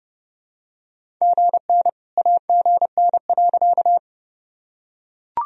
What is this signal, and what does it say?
Morse code, heard as a keyed tone, about 32 wpm, 700 Hz. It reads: GN AGN.
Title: Sentences from Top 300 Words and Abbreviations 30wpm